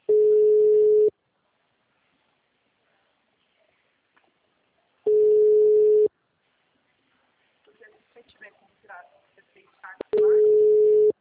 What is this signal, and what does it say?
Ringback tone (Europe, 425 Hz), with speech.